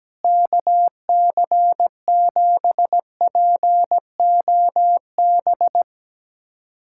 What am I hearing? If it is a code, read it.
Morse code, heard as a keyed tone, about 17 wpm, 700 Hz. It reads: KC7POB
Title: USA Callsign DE Callsign - 6 Characters - 17wpm